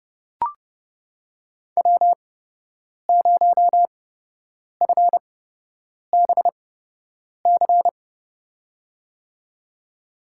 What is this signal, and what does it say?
Morse code, heard as a keyed tone, about 32 wpm, spaced as for 8 wpm, 700 Hz. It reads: W0FBC
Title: USA Call Signs - 5 Characters - 8x Character Spacing - 30wpm